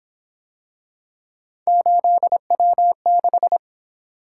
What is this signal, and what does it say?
Morse code, heard as a keyed tone, about 26 wpm, 700 Hz. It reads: 8W6
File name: Speed-Racing - Three Letter-Number Combinations - 22wpm